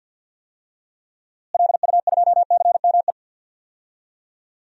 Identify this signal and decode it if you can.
Morse code, heard as a keyed tone, about 46 wpm, 700 Hz. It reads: LU2CME
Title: DXCC-100 Call Signs - 6 Characters - Repeated 3x - 50wpm